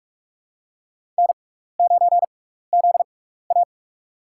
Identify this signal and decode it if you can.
Morse code, heard as a keyed tone, about 46 wpm, spaced as for 15 wpm, 700 Hz. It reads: N9ZA
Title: USA Call Signs - 4 Characters - 6x Character Spacing - 45wpm